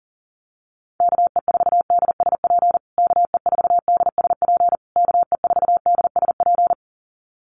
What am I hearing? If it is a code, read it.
Morse code, heard as a keyed tone, about 40 wpm, 700 Hz. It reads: KE4DSP KE4DSP KE4DSP
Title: USA Call Signs Repeated 3x 40wpm